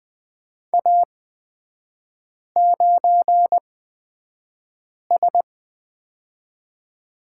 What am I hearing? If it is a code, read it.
Morse code, heard as a keyed tone, about 20 wpm, spaced as for 5 wpm, 700 Hz. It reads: A9S
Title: 236 Lesson 38e - Letter q - Three Characters 20wpm